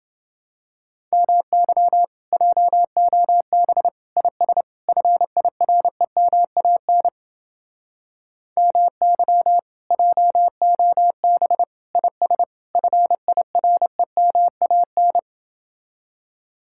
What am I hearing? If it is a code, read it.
Morse code, heard as a keyed tone, about 29 wpm, 700 Hz. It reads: MY JOB IS FIREMAN MY JOB IS FIREMAN